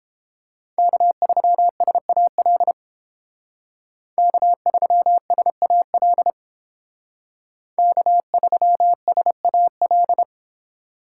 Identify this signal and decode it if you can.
Morse code, heard as a keyed tone, about 30 wpm, 700 Hz. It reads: K3SAL K3SAL K3SAL